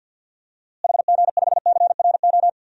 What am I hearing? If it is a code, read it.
Morse code, heard as a keyed tone, about 50 wpm, 700 Hz. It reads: HG5CRO
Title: DXCC-100 Call Signs - 6 Characters - Repeated 3x - 50wpm